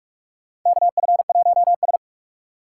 Morse code, heard as a keyed tone, about 44 wpm, 700 Hz. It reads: KF1S
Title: USA Call Signs - 2x Character Spacing - 45wpm